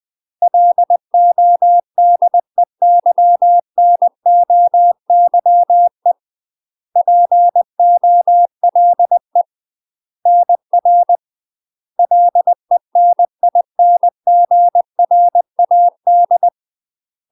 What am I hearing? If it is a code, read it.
Morse code, heard as a keyed tone, about 20 wpm, 700 Hz. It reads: LODEYNOYE POLE NR LENINGRAD
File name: UK MASTER QSOs TRAINING / 20WPM / UK MASTER QSOs 0017